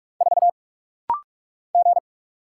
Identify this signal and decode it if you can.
Morse code, heard as a keyed tone, about 44 wpm, 700 Hz. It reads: 4 G
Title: Single Letter-Number Alt - Rapid-Fire 45wpm